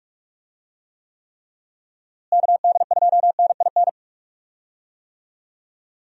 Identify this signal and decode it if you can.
Morse code, heard as a keyed tone, about 45 wpm, 700 Hz. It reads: KD2NIN
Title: USA Call Signs - 6 Characters - Repeated 3x - 45wpm